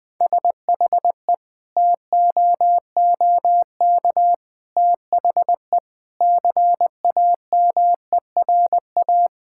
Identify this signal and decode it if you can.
Morse code, heard as a keyed tone, about 20 wpm, 700 Hz. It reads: SHE TOOK THE CAMERA
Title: Sentences from Top 1000 Words - Review Part 5 - 20wpm